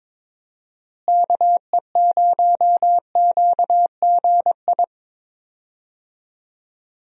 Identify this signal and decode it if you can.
Morse code, heard as a keyed tone, about 21 wpm, 700 Hz. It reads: KE0QGI
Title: USA Call Signs - 6 Characters - Repeated 2x - 22wpm